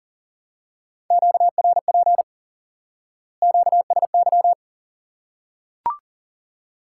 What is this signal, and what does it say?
Morse code, heard as a keyed tone, about 38 wpm, 700 Hz. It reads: QRP QSY
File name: IQE Course 3 - Lesson 89 - YAGI - Two Elements - 40wpm